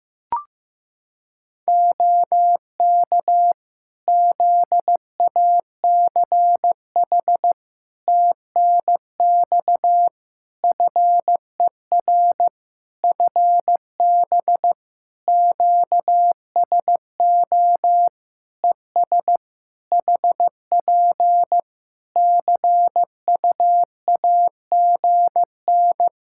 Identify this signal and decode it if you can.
Morse code, heard as a keyed tone, about 16 wpm, 700 Hz. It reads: OK ZACH TNX FER FB QSO ES HP CUAGN
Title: K1USN SST QSO Practice 15wpm